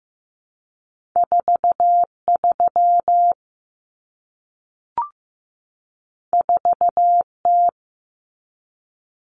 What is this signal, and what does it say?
Morse code, heard as a keyed tone, about 15 wpm, 700 Hz. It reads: 43 4T
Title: Common Cut Numbers - 2-Digit 15wpm